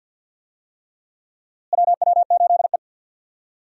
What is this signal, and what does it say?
Morse code, heard as a keyed tone, about 46 wpm, 700 Hz. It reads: WW8E
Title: USA Call Signs - 4 Characters - Repeated 3x - 50wpm